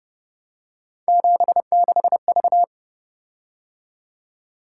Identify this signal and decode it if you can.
Morse code, heard as a keyed tone, about 29 wpm, 700 Hz. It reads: 76V